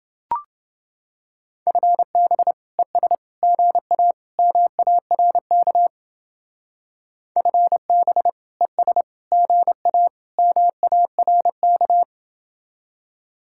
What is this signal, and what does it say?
Morse code, heard as a keyed tone, about 30 wpm, 700 Hz. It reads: FB ES GA MARK FB ES GA MARK